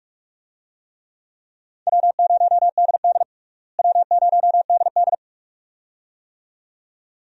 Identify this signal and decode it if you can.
Morse code, heard as a keyed tone, about 46 wpm, 700 Hz. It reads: W0DD W0DD
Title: USA Call Signs - 4 Characters - Repeated 2x - 45wpm